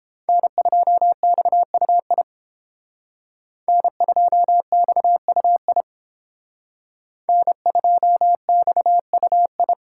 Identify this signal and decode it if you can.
Morse code, heard as a keyed tone, about 29 wpm, 700 Hz. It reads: N2XUI N2XUI N2XUI